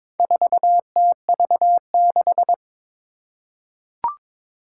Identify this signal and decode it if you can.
Morse code, heard as a keyed tone, about 22 wpm, 700 Hz. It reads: 4TV6
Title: Four Challenging Letter-Number Combinations 22wpm